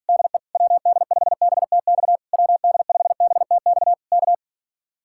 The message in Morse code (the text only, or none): DE WD5BTX WD5BTX K